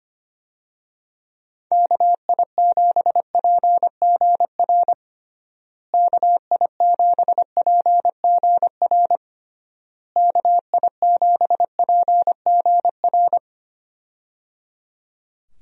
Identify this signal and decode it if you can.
Morse code, heard as a keyed tone, about 25 wpm, 700 Hz. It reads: KI7PGR KI7PGR KI7PGR
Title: USA Call Signs - 6 Characters - Repeated 3x - 25wpm